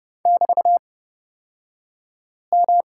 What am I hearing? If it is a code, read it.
Morse code, heard as a keyed tone, about 30 wpm, spaced as for 5 wpm, 700 Hz. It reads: <BT>M
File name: Lesson 36d - BT - Two Characters 30wpm